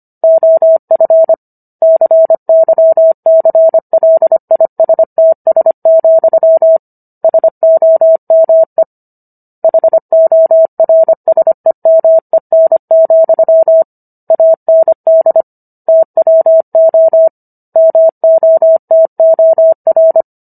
Morse code, heard as a keyed tone, about 25 wpm, 650 Hz. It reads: OF CYCLISTS, SOME HORSEMEN, AND TWO MOTOR